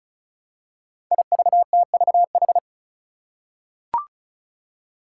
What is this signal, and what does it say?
Morse code, heard as a keyed tone, about 36 wpm, 700 Hz. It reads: IVTVH
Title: Five Challenging Letter-Number Combinations 35wpm